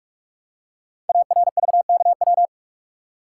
Morse code, heard as a keyed tone, about 44 wpm, 690 Hz. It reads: ARVKW
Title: Code Groups - 5 Letters - 45wpm